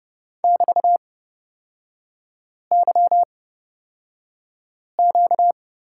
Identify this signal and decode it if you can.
Morse code, heard as a keyed tone, about 29 wpm, spaced as for 5 wpm, 700 Hz. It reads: <BT>YQ
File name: Lesson 38e - Letter q - Three Characters 30wpm